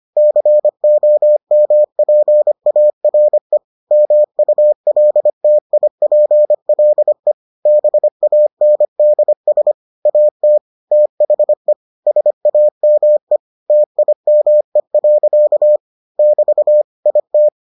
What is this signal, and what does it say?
Morse code, heard as a keyed tone, about 25 wpm, 600 Hz. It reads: COMPARE MULTIPLE BANDS AT THE SAME TIME. <BT> IT